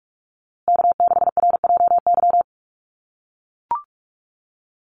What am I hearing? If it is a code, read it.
Morse code, heard as a keyed tone, about 44 wpm, 700 Hz. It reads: K6RJY
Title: USA Call Signs Repeated 2x 45wpm